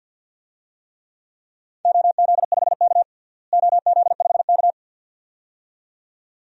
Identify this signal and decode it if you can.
Morse code, heard as a keyed tone, about 49 wpm, 700 Hz. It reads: OZ5K OZ5K
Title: DXCC-100 Call Signs - 4 Characters - Repeated 2x - 50wpm